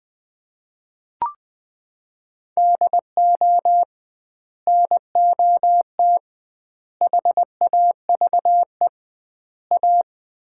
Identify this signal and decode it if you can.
Morse code, heard as a keyed tone, about 21 wpm, 700 Hz. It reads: DO NOT HAVE A
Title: 2x Word-Spacing - Sets of 4 Words - 20wpm